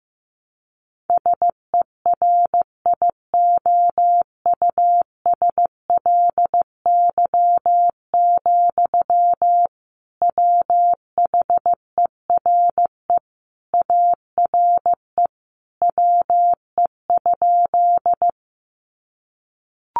Morse code, heard as a keyed tone, about 14 wpm, 700 Hz. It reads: SERIOUSLY, WHERE ARE WE?
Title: Sentences from Top 1200 Words 15wpm